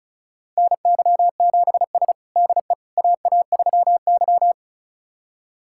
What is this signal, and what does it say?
Morse code, heard as a keyed tone, about 34 wpm, 700 Hz. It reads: NY7S DE AA3Y
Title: USA Callsign DE Callsign - 4 Characters - 35wpm